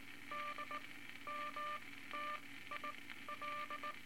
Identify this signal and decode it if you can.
Morse code, heard as a keyed tone, about 18 wpm, spaced as for 13 wpm, 1190 Hz, with noise. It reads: DMTIL
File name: Morse Code Records / 18 A LCM 164 Ten Words Per Minute